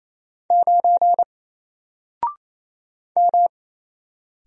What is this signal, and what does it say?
Morse code, heard as a keyed tone, about 27 wpm, 700 Hz. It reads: 9 M
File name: Single-Letter-Number-28wpm